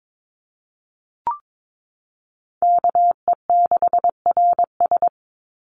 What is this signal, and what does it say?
Morse code, heard as a keyed tone, about 22 wpm, 700 Hz. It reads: KE6RS
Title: NA SOTA Chaser Callsigns 22wpm